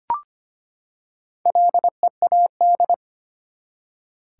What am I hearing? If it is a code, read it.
Morse code, heard as a keyed tone, about 25 wpm, 700 Hz. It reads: LEAD
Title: Four Letter Words - 25wpm